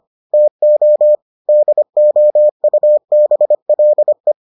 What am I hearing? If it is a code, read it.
Morse code, heard as a keyed tone, about 25 wpm, 600 Hz. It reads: TO DOUBLE